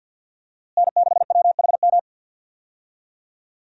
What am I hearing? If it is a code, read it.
Morse code, heard as a keyed tone, about 52 wpm, 700 Hz. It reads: N6WHM
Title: USA Call Signs - 5 Characters - Repeated 2x - 50wpm